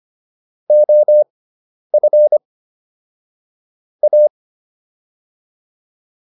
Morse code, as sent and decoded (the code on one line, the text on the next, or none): --- ..-. / .-
OF A